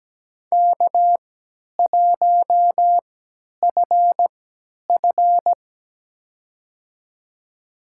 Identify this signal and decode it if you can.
Morse code, heard as a keyed tone, about 17 wpm, spaced as for 10 wpm, 700 Hz. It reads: K1FF
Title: USA Call Signs - 4 Characters - 3x Character Spacing - 17wpm